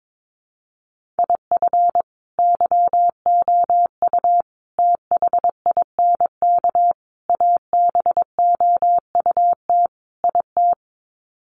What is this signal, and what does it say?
Morse code, heard as a keyed tone, about 22 wpm, 700 Hz. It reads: IF YOU THINK ABOUT IT
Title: Sets of 5 Words Encore2 22wpm